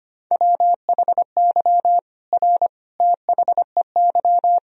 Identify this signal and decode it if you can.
Morse code, heard as a keyed tone, about 25 wpm, 700 Hz. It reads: WHY R THEY
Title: Sentences from Top 200 Words and Abbreviations 25wpm